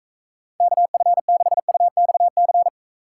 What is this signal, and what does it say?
Morse code, heard as a keyed tone, about 44 wpm, 700 Hz. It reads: KF6UXC